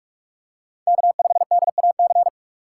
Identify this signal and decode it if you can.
Morse code, heard as a keyed tone, about 45 wpm, 700 Hz. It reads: K5DAC
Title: USA Call Signs - 5 Characters - 2x Character Spacing - 45wpm